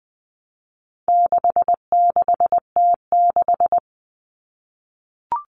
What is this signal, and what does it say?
Morse code, heard as a keyed tone, about 20 wpm, 700 Hz. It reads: 66T6
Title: Common Cut Numbers - 4-Digit 20wpm